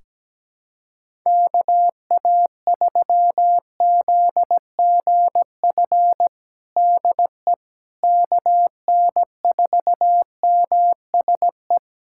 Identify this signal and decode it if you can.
Morse code, heard as a keyed tone, about 17 wpm, 700 Hz. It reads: KA3ZGF DE KN4MSE
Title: USA Callsign DE Callsign 17wpm